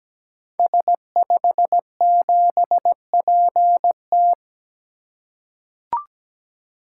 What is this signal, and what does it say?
Morse code, heard as a keyed tone, about 17 wpm, 700 Hz. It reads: S57PT
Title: Code Groups - 5 Letter:Numbers - 17wpm